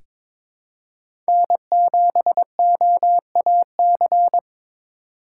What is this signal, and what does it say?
Morse code, heard as a keyed tone, about 22 wpm, 700 Hz. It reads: N7OAC
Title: USA Call Signs - 5 Characters - 2x Character Spacing - 22wpm